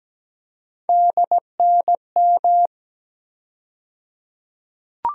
Morse code, heard as a keyed tone, about 17 wpm, 700 Hz. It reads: DNM
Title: USA Call Sign Postfixes - 3 Letters 17wpm